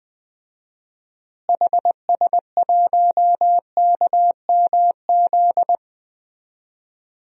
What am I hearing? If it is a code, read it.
Morse code, heard as a keyed tone, about 20 wpm, 700 Hz. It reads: HS1KMZ
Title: DXCC-100 Call Signs - 6 Characters - Repeated 2x - 20wpm